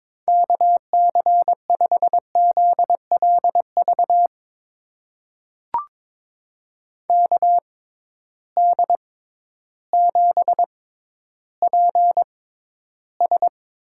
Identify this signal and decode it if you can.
Morse code, heard as a keyed tone, about 22 wpm, spaced as for 12 wpm, 700 Hz. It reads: KC5ZLV KD7PS
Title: USA Call Signs - 6 Characters - 6x Character Spacing - 22wpm